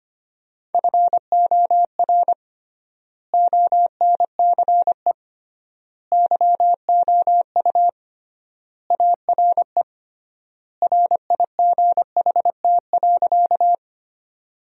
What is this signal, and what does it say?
Morse code, heard as a keyed tone, about 25 wpm, 700 Hz. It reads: FOR ONCE YOU ARE RIGHT.
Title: 3x Word-Spacing - Sentences from Top 300 Words - 25wpm